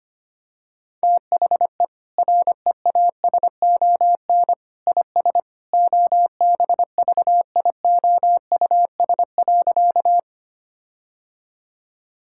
Morse code, heard as a keyed tone, about 25 wpm, 700 Hz. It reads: THE REASON IS OBVIOUS.
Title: Sentences from Top 1600 Words 25wpm